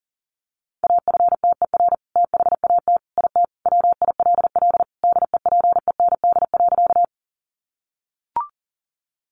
Morse code, heard as a keyed tone, about 40 wpm, 700 Hz. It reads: AFTER THAT IT WILL DEPEND.
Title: Sentences from Top 1300 Words 40wpm